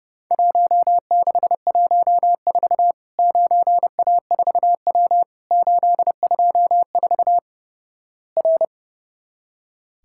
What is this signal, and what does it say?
Morse code, heard as a keyed tone, about 32 wpm, 700 Hz. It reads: 1614 9A4W 824 R
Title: Worked All Europe DX Contest - QTC Exchange - 30wpm